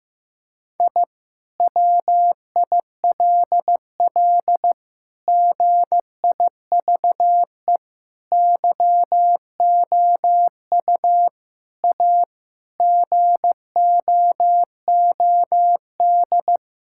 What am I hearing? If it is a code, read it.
Morse code, heard as a keyed tone, about 15 wpm, 700 Hz. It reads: I WILL GIVE YOU A GOOD
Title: Sentences from Top 1000 Words - Review Part 6 - 15wpm